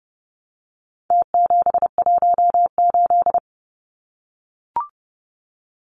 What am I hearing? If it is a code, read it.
Morse code, heard as a keyed tone, about 30 wpm, 700 Hz. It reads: T718
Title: Numbers - Contest Style - 4-Digit 30wpm